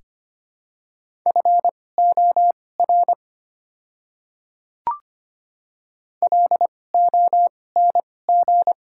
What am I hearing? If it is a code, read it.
Morse code, heard as a keyed tone, about 25 wpm, spaced as for 18 wpm, 700 Hz. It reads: FOR LONG
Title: 2x Character Spacing - Top 300 Words - 25wpm